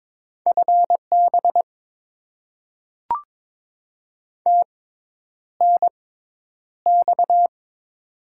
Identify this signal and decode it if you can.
Morse code, heard as a keyed tone, about 22 wpm, spaced as for 10 wpm, 700 Hz. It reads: FB TNX
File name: Top 20 QSO Elements - 6x Character Spacing - 22wpm